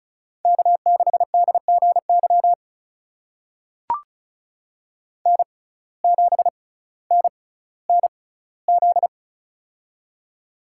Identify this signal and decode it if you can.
Morse code, heard as a keyed tone, about 36 wpm, spaced as for 18 wpm, 700 Hz. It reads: K6DGY N7NNZ
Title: USA Call Signs - 5 Characters - 6x Character Spacing - 35wpm